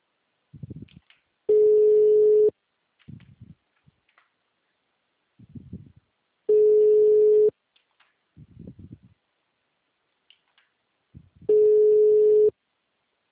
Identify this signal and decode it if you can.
Ringback tone (Europe, 425 Hz).